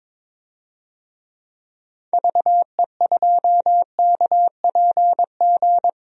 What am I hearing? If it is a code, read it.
Morse code, heard as a keyed tone, about 22 wpm, 700 Hz. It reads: VE2KPG
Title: DXCC-100 Call Signs - 6 Characters - 22wpm